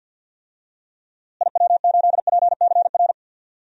Morse code, heard as a keyed tone, about 46 wpm, 700 Hz. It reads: IW8PCR
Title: DXCC-100 Call Signs - 6 Characters - Repeated 3x - 50wpm